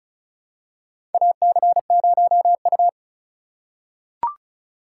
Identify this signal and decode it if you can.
Morse code, heard as a keyed tone, about 36 wpm, 700 Hz. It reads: AC0U
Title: USA Call Signs - 4 Characters - 3x Character Spacing - 35wpm